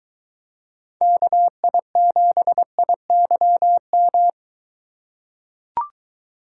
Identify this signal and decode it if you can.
Morse code, heard as a keyed tone, about 22 wpm, 700 Hz. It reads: KI7IYM